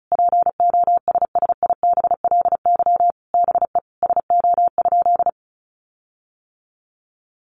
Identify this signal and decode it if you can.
Morse code, heard as a keyed tone, about 34 wpm, 700 Hz. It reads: POSSIBLY BE SO?
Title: Sentences from Top 1300 Words 35wpm